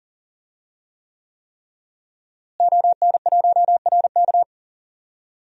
Morse code, heard as a keyed tone, about 40 wpm, 700 Hz. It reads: ON1RK
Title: DXCC-100 Call Signs - 5 Characters - Repeated 3x - 40wpm